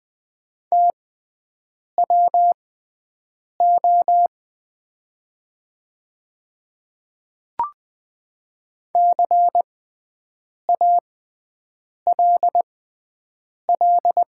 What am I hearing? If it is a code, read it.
Morse code, heard as a keyed tone, about 20 wpm, spaced as for 7 wpm, 700 Hz. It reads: TWO CALL
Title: 6x Character Spacing - Top 200 Words - 20wpm